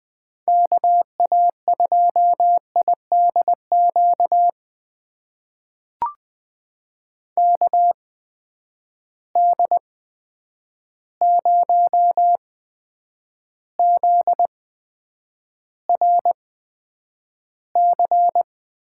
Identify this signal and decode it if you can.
Morse code, heard as a keyed tone, about 20 wpm, spaced as for 9 wpm, 700 Hz. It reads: KA2IDQ KD0ZRC
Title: USA Call Signs - 6 Characters - 8x Character Spacing - 20wpm